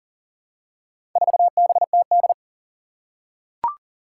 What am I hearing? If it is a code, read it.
Morse code, heard as a keyed tone, about 40 wpm, 700 Hz. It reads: 4BTD
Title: Four Challenging Letter-Number Combinations 40wpm